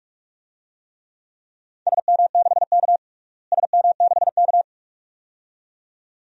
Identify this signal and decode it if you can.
Morse code, heard as a keyed tone, about 42 wpm, 700 Hz. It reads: SM6K SM6K
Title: DXCC-100 Call Signs - 4 Characters - Repeated 2x - 45wpm